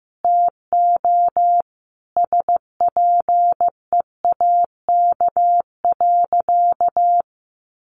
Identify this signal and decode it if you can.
Morse code, heard as a keyed tone, about 15 wpm, 700 Hz. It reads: TO SPEAK.
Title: Sentences from Top 1100 Words 15wpm